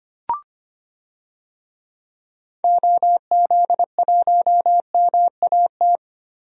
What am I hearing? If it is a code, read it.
Morse code, heard as a keyed tone, about 25 wpm, 700 Hz. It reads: OZ1MAT
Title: DXCC-100 Call Signs - 6 Characters - 25wpm